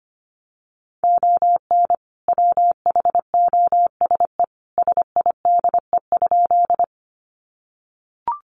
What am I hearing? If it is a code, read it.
Morse code, heard as a keyed tone, about 25 wpm, 700 Hz. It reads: ON WHOSE SIDE?
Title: Sentences from Top 1200 Words 25wpm